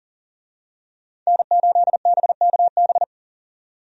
Morse code, heard as a keyed tone, about 40 wpm, 700 Hz. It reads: N8BKB
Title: USA Call Signs - 5 Characters - 40wpm